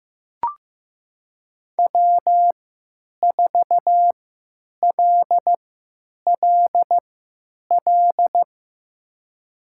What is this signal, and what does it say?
Morse code, heard as a keyed tone, about 15 wpm, spaced as for 9 wpm, 700 Hz. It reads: W4LLL
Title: USA Call Signs - 5 Characters - 3x Character Spacing - 15wpm